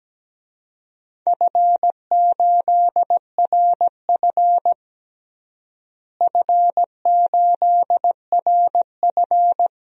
Morse code, heard as a keyed tone, about 17 wpm, 700 Hz. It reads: F8RF F8RF
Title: DXCC-100 Call Signs - 4 Characters - Repeated 3x - 17wpm